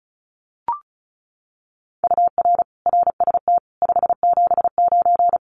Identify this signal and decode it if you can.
Morse code, heard as a keyed tone, about 34 wpm, 700 Hz. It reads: UR RST 579
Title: Common QSO Phrases Encore 35wpm